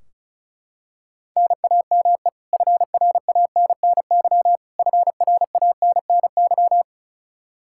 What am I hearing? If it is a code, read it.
Morse code, heard as a keyed tone, about 34 wpm, 700 Hz. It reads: NAME FRANNY FRANNY
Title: Practice QSO - Exchange 1 - Beginning - 35wpm